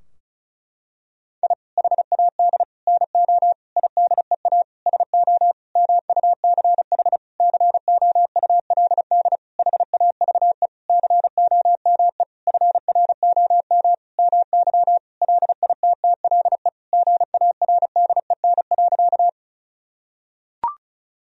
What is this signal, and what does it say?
Morse code, heard as a keyed tone, about 36 wpm, 700 Hz. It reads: I HAD NO IDEA SO MUCH COULD HAVE COME FROM MY LITTLE GARDEN.
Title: Sentences from Top 1000 Words - Review Part 4 - 35wpm